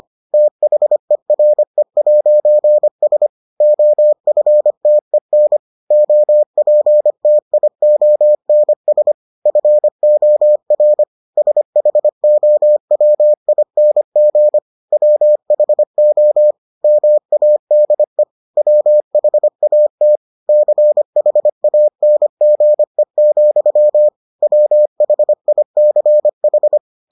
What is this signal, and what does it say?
Morse code, heard as a keyed tone, about 25 wpm, 600 Hz. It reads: THERE'S OFTEN OPTIONS FOR SHOWING WHO MADE WHAT CHANGE, WHICH